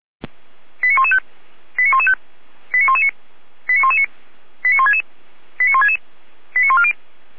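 Five-tone selective call (ZVEI), heard as quick runs of five tones, tones of about 70 ms. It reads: 89115 89115 89119 89119 89166 89166 89140
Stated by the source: ZVEI five-tone selective call received with an SDR.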